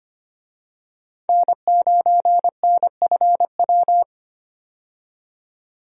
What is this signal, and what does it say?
Morse code, heard as a keyed tone, about 25 wpm, 690 Hz. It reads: N9NFW
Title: USA Call Signs - 5 Characters - 25wpm